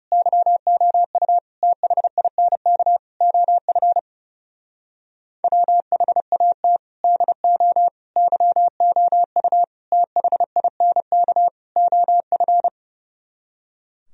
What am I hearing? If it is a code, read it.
Morse code, heard as a keyed tone, about 34 wpm, 700 Hz. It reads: YOU THINK OF WHAT DO YOU THINK OF
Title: Speed-Racing - Sets of 5 Words 30wpm